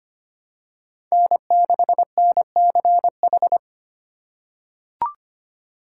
Morse code, heard as a keyed tone, about 25 wpm, 700 Hz. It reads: N6NCH
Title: USA Call Signs - 5 Characters - 3x Character Spacing - 25wpm